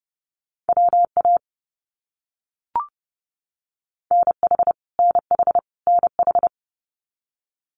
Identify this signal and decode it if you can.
Morse code, heard as a keyed tone, about 30 wpm, 700 Hz. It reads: WA NH NH NH
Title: US States Repeated 3x 30wpm